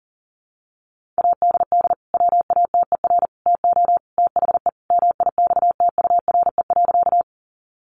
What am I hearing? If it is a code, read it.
Morse code, heard as a keyed tone, about 40 wpm, 700 Hz. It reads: ADD WATER TO THE MIXTURE.
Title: Sentences from Top 1500 Words 40wpm